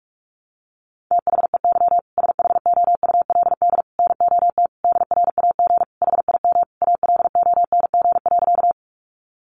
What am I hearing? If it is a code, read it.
Morse code, heard as a keyed tone, about 44 wpm, 700 Hz. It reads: THEY SHOULD NOT DRAG HIM ALONG.
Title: Sentences from Top 1500 Words 45wpm